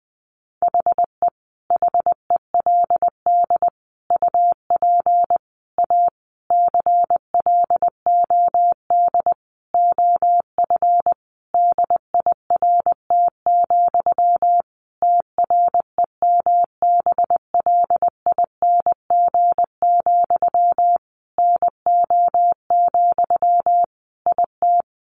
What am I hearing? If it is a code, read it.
Morse code, heard as a keyed tone, about 20 wpm, 700 Hz. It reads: HE HELD UP A CLOD OF DIRT, TREMBLING, NO, IT
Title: A Sound Of Thunder 20wpm